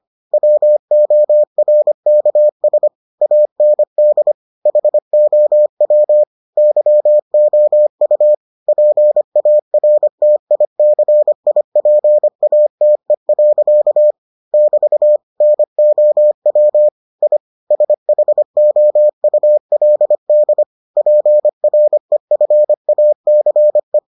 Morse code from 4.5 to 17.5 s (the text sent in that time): HOW YOU PARTICIPATE. <BT> NOW I